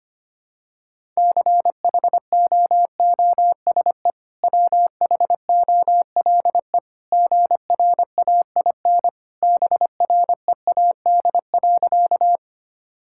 Morse code, heard as a keyed tone, about 25 wpm, 700 Hz. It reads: CHOOSE WHOLE GRAIN BREAD.
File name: Easy Sentences with Hints - 25wpm